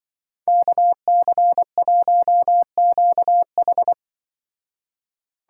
Morse code, heard as a keyed tone, about 24 wpm, 700 Hz. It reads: KC1QH
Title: Speed Racing - USA Call Signs 20wpm